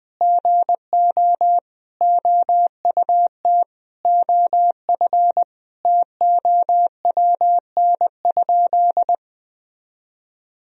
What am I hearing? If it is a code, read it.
Morse code, heard as a keyed tone, about 21 wpm, 700 Hz. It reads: GO OUT OF TOWN?
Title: Sentences from Top 1000 Words - Review Part 3 - 20wpm